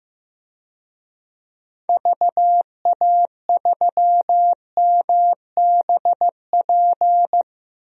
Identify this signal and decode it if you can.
Morse code, heard as a keyed tone, about 15 wpm, 700 Hz. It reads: VA3MBP
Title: DXCC-100 Call Signs - 6 Characters - Repeated 2x - 15wpm